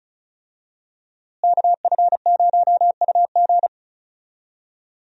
Morse code, heard as a keyed tone, about 34 wpm, 700 Hz. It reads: KF0UG
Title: USA Call Signs - 5 Characters - Repeated 3x - 35wpm